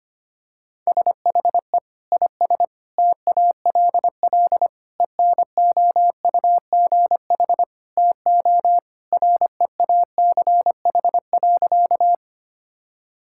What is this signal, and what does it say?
Morse code, heard as a keyed tone, about 25 wpm, 700 Hz. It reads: SHE IS TALL ENOUGH TO REACH.
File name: practice / 25wpm / Sentences with 2-Word Phrases - 25wpm